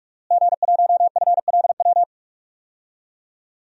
Morse code, heard as a keyed tone, about 41 wpm, 700 Hz. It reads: G1FLW
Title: DXCC-100 Call Signs - 5 Characters - Repeated 3x - 45wpm